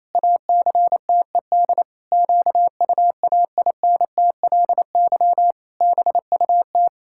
Morse code, heard as a keyed tone, about 28 wpm, 700 Hz. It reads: ACTED QUAINTLY BUT
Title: Pangram Challenge v2 - 28wpm